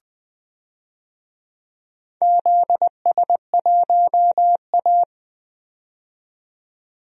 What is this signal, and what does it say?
Morse code, heard as a keyed tone, about 20 wpm, 700 Hz. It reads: ZS1A
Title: DXCC-100 Call Signs - 4 Characters - 20wpm